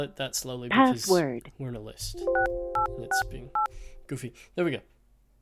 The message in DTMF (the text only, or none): 6868